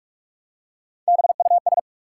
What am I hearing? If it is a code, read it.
Morse code, heard as a keyed tone, about 46 wpm, 700 Hz. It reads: BUS
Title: Three Letter Words - 45wpm